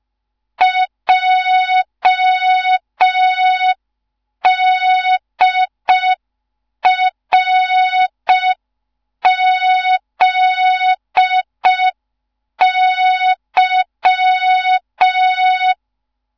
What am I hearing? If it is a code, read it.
Morse code, heard as a keyed tone, about 5 wpm, 3700 Hz. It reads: JDRZY
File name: Morze-HA2MN / hang / 15 GYAKORLAT